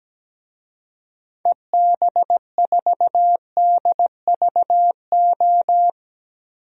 Morse code, heard as a keyed tone, about 17 wpm, 700 Hz. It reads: EB4DVO
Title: DXCC-100 Call Signs - 6 Characters - 17wpm